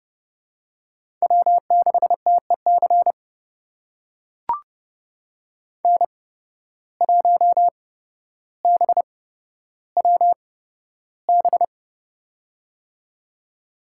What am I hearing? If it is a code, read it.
Morse code, heard as a keyed tone, about 30 wpm, spaced as for 13 wpm, 700 Hz. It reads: W6TEC N1BWB
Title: USA Call Signs - 5 Characters - 8x Character Spacing - 30wpm